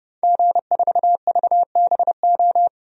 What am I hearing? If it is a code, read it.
Morse code, heard as a keyed tone, about 30 wpm, 700 Hz. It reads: G4VBO
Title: DXCC-100 Call Signs - 5 Characters - Repeated 3x - 30wpm